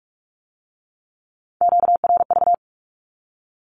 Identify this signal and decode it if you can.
Morse code, heard as a keyed tone, about 46 wpm, 700 Hz. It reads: QRV
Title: Q-Codes with Spoken Meaning - 45wpm